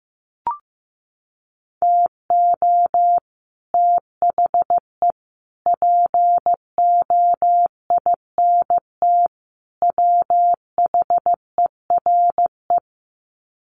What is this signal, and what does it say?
Morse code, heard as a keyed tone, about 15 wpm, 700 Hz. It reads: TO THE POINT WHERE
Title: Sets of 4 Words 15wpm